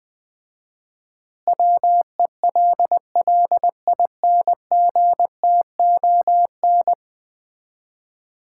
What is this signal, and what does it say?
Morse code, heard as a keyed tone, about 20 wpm, 700 Hz. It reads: WELLINGTON
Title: Common QSO Phrases - 4x Word-Spacing - 20wpm